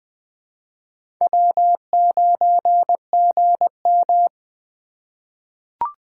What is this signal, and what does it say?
Morse code, heard as a keyed tone, about 21 wpm, 700 Hz. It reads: W9GM
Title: 2023 POTA Hunter Call Signs - 20wpm